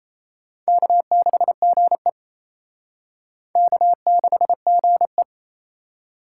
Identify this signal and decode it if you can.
Morse code, heard as a keyed tone, about 35 wpm, 700 Hz. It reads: K6GE K6GE